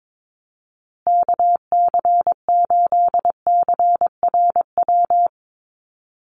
Morse code, heard as a keyed tone, about 22 wpm, 700 Hz. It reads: KC8CRW
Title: USA Call Signs Repeated 2x 22wpm